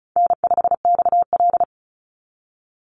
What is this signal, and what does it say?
Morse code, heard as a keyed tone, about 34 wpm, 700 Hz. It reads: N5XL
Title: NA SOTA Chaser Callsigns 35wpm